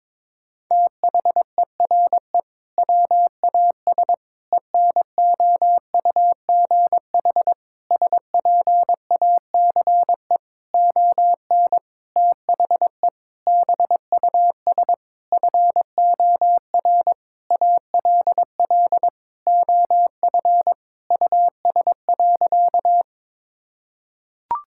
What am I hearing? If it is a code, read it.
Morse code, heard as a keyed tone, about 22 wpm, 700 Hz. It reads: THERE WAS ENOUGH SPACE ON THE BUS FOR ALL OF US.
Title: Sentences from Top 1000 Words - Review Part 6 - 22wpm